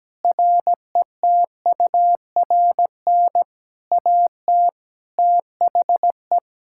Morse code, heard as a keyed tone, about 17 wpm, 700 Hz. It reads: RETURN AT THE
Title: Sentences from Top 1600 Words 17wpm